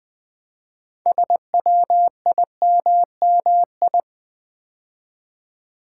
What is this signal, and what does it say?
Morse code, heard as a keyed tone, about 20 wpm, 700 Hz. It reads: SWIMMI
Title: Build A Word - 2 Letters At A Time - 8-Letter Words - 20wpm